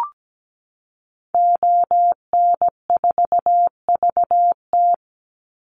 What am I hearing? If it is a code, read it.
Morse code, heard as a keyed tone, about 17 wpm, 700 Hz. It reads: ON4VT
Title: Call Signs - 17wpm